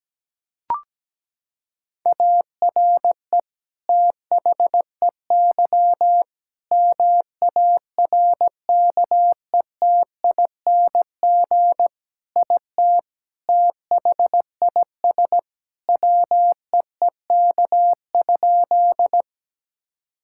Morse code, heard as a keyed tone, about 17 wpm, 700 Hz. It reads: ARE THEY MARKETING IT THIS WEEK?
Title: Sentences from Top 1000 Words - Review Part 5 - 17wpm